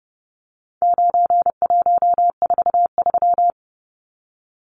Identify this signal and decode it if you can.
Morse code, heard as a keyed tone, about 30 wpm, 700 Hz. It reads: 9143
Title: Numbers - 4-Digit 30wpm